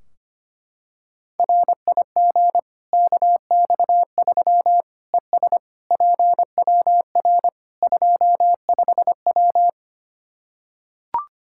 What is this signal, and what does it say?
Morse code, heard as a keyed tone, about 25 wpm, 700 Hz. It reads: RIG KX3 ES PWR 25W
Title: Practice QSO - Exchange 2 - Info - 25wpm